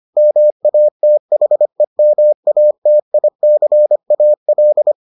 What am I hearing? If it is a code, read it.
Morse code, heard as a keyed tone, about 25 wpm, 600 Hz. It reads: MATHEMATICAL